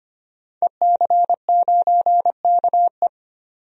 Morse code, heard as a keyed tone, about 25 wpm, 700 Hz. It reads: EC9KE
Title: DXCC-100 Call Signs - 5 Characters - 25wpm